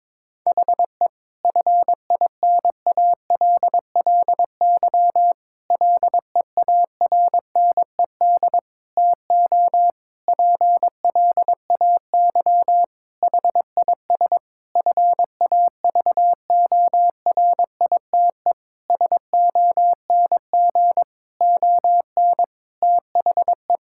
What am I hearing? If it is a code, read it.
Morse code, heard as a keyed tone, about 22 wpm, 700 Hz. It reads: HE FINALLY LEARNED TO PLAY HIS FAVORITE SONG ON THE
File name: Difficult Sentences with Hints - 22wpm